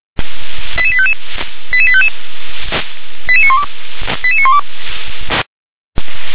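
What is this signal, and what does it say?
Five-tone selective call (ZVEI), heard as quick runs of five tones, tones of about 70 ms. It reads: 88855 88855 88912 88912
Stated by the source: ZVEI five-tone call 88702 received by SDR.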